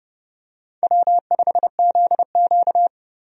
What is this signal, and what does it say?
Morse code, heard as a keyed tone, about 30 wpm, 700 Hz. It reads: W5ZQ
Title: USA Call Signs - 4 Characters - 30wpm